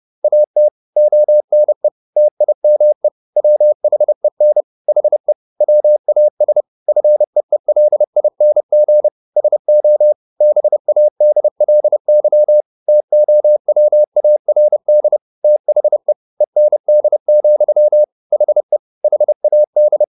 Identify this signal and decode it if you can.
Morse code, heard as a keyed tone, about 30 wpm, 600 Hz. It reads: AT ONE TIME WHEN HE WAS FEELING SO BADLY TOWARD THE END, HE HAD